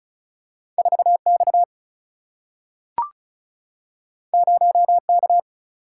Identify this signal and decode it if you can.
Morse code, heard as a keyed tone, about 36 wpm, 700 Hz. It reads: 4 X 0 K
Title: Single-Letter Single-Number 35wpm